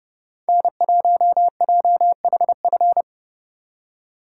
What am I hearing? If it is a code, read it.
Morse code, heard as a keyed tone, about 28 wpm, 700 Hz. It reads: N1JHF
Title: Speed Racing - USA Call Signs 30wpm